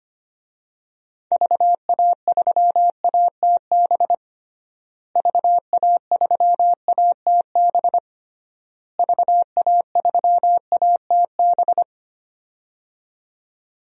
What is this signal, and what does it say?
Morse code, heard as a keyed tone, about 25 wpm, 700 Hz. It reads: VA3ATB VA3ATB VA3ATB
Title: DXCC-100 Call Signs - 6 Characters - Repeated 3x - 25wpm